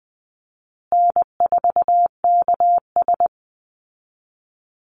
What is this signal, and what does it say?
Morse code, heard as a keyed tone, about 20 wpm, 700 Hz. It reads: N4KS
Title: Easy Call Signs Repeated 2x - 20wpm